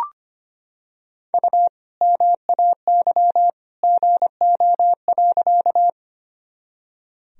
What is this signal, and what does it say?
Morse code, heard as a keyed tone, about 25 wpm, 700 Hz. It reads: U MAY GO.
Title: Sentences from Top 100 Words and Abbreviations 25wpm